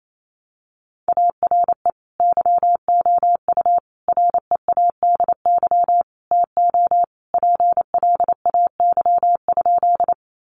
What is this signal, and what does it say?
Morse code, heard as a keyed tone, about 28 wpm, 700 Hz. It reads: ARE YOU READY TO PLAY?
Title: Sentences from Top 400 Words 28wpm